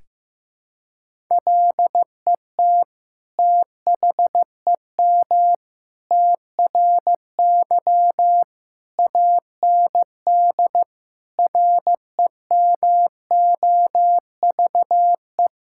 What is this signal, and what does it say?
Morse code, heard as a keyed tone, about 15 wpm, 700 Hz. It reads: LET THEM TRY AND REMOVE
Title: Sentences from Top 1000 Words - Review Part 5 - 15wpm